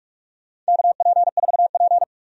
Morse code, heard as a keyed tone, about 47 wpm, 700 Hz. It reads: KP4P
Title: USA Call Signs - 4 Characters - 45wpm